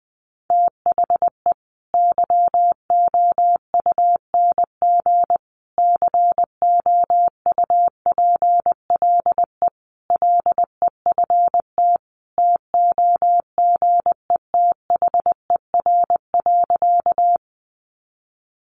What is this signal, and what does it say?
Morse code, heard as a keyed tone, about 19 wpm, 700 Hz. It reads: THE YOUNG COUPLE LEFT TOGETHER.
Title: Sentences from Top 1100 Words 20wpm